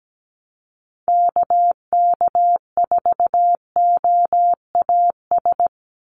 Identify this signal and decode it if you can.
Morse code, heard as a keyed tone, about 17 wpm, 700 Hz. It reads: KK4OAS
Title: USA Call Signs 17wpm